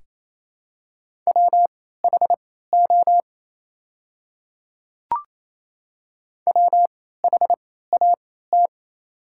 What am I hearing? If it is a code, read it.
Morse code, heard as a keyed tone, about 28 wpm, spaced as for 16 wpm, 690 Hz. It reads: WHO WHAT
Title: 3x Character Spacing - Top 100 Words - 28wpm